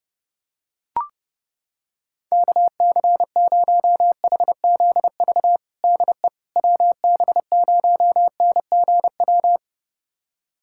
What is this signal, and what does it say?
Morse code, heard as a keyed tone, about 30 wpm, 700 Hz. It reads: KC0HZV DE WB0NGW
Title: USA Callsign DE Callsign - 6 Characters - 30wpm